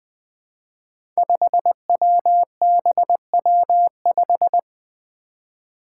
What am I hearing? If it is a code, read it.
Morse code, heard as a keyed tone, about 21 wpm, 700 Hz. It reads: 5WBW5
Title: Five Challenging Letter-Number Combinations 20wpm